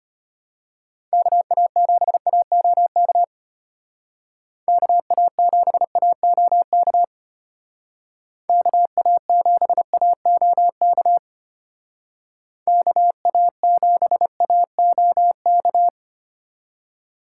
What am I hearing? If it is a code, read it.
Morse code, heard as a keyed tone, about 30 wpm, 700 Hz. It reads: KA7AOK KA7AOK KA7AOK KA7AOK
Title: Speed Racing - USA Call Signs 25wpm